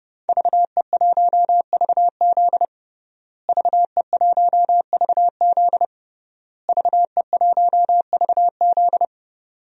Morse code, heard as a keyed tone, about 30 wpm, 700 Hz. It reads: VE1VZ VE1VZ VE1VZ
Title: DXCC-100 Call Signs - 5 Characters - Repeated 3x - 30wpm